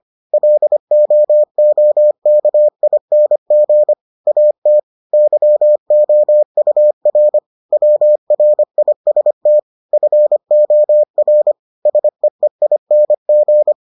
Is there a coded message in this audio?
Yes: Morse code from 0.3 to 13.7 s, heard as a keyed tone, about 25 wpm, 600 Hz. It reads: LOOKING AT YOUR WRIST FOR SEEING